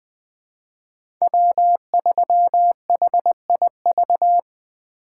Morse code, heard as a keyed tone, about 20 wpm, 700 Hz. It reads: W3HIV
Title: USA Call Signs - 5 Characters - 4x Character Spacing - 20wpm